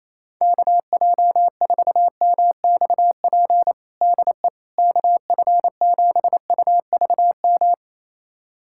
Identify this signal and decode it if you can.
Morse code, heard as a keyed tone, about 27 wpm, 700 Hz. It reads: KJ4MXP DE KF7UVM
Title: USA Callsign DE Callsign 28wpm